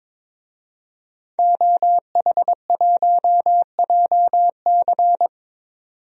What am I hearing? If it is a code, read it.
Morse code, heard as a keyed tone, about 22 wpm, 700 Hz. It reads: OH1JC
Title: DXCC-100 Call Signs - 5 Characters - 22wpm